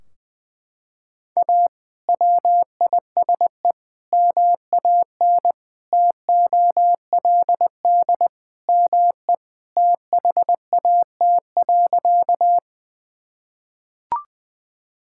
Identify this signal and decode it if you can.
Morse code, heard as a keyed tone, about 21 wpm, 700 Hz. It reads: A WISE MAN TOLD ME THAT.
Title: Sentences from Top 1600 Words 20wpm